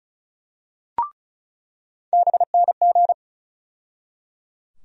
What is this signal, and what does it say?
Morse code, heard as a keyed tone, about 36 wpm, 700 Hz. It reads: BNG
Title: USA Call Sign Postfixes - 3 Letters 35wpm